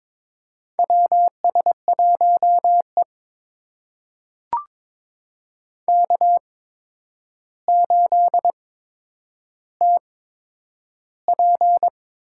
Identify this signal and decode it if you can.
Morse code, heard as a keyed tone, about 23 wpm, spaced as for 9 wpm, 700 Hz. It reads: WS1E K8TP
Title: USA Call Signs - 4 Characters - 8x Character Spacing - 22wpm